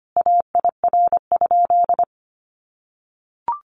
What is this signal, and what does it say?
Morse code, heard as a keyed tone, about 25 wpm, 700 Hz. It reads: AIR?
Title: Sentences from Top 1200 Words 25wpm